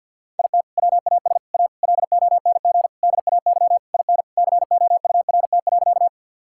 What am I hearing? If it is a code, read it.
Morse code, heard as a keyed tone, about 50 wpm, 700 Hz. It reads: IT WAS A LONG DAY IN COURT.
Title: Sentences from Top 1600 Words 50wpm